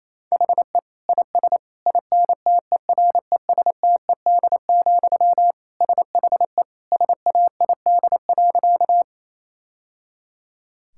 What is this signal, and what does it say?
Morse code, heard as a keyed tone, about 28 wpm, 700 Hz. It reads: HE IS INTERESTED, SHE SAID.
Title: Sentences from Top 1000 Words - Review Part 6 - 28wpm